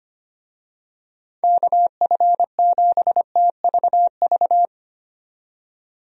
Morse code, heard as a keyed tone, about 25 wpm, 700 Hz. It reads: KF7TVV
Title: USA Call Signs - 6 Characters - 25wpm